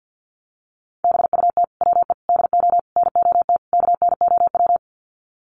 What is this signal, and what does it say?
Morse code, heard as a keyed tone, about 48 wpm, 700 Hz. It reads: BUT WE DO NOT KNOW
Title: Sets of 5 Words Encore2 50wpm